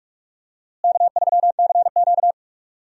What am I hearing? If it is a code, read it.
Morse code, heard as a keyed tone, about 43 wpm, 700 Hz. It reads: K3CQ
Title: USA Call Signs - 4 Characters - Repeated 3x - 45wpm